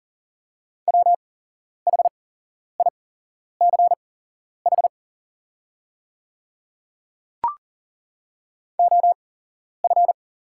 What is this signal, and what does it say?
Morse code, heard as a keyed tone, about 40 wpm, spaced as for 11 wpm, 690 Hz. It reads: WHICH OF
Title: 8x Character Spacing - Top 100 Words - 40wpm